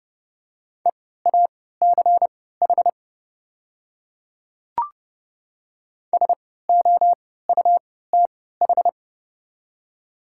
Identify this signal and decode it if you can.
Morse code, heard as a keyed tone, about 30 wpm, spaced as for 17 wpm, 700 Hz. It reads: EACH SOUTH
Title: 3x Character Spacing - Top 500 Words - 30wpm